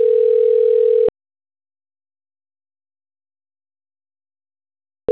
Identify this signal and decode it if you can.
Ringback tone (North America), 440 + 480 Hz.